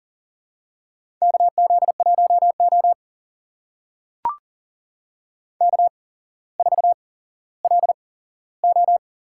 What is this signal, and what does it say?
Morse code, heard as a keyed tone, about 38 wpm, spaced as for 17 wpm, 700 Hz. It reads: KZ1O K4LO
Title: USA Call Signs - 4 Characters - 8x Character Spacing - 40wpm